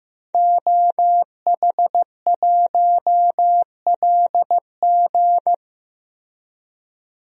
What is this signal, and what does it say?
Morse code, heard as a keyed tone, about 15 wpm, 700 Hz. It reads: OH1LG
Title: DXCC-100 Call Signs - 5 Characters - 15wpm